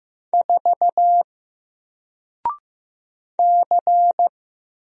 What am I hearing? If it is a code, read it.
Morse code, heard as a keyed tone, about 15 wpm, 700 Hz. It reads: 4 C
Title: Single Character (Letter:Number:Punctuation:Prosign) 15wpm